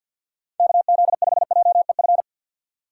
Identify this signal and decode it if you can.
Morse code, heard as a keyed tone, about 52 wpm, 700 Hz. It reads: KZ5JEF
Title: USA Call Signs - 6x Character Spacing - 50wpm